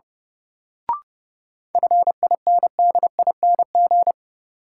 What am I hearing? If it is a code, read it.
Morse code, heard as a keyed tone, about 30 wpm, 700 Hz. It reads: FINDING
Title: Sentences from Top 1100 Words - New Word Drill - 30wpm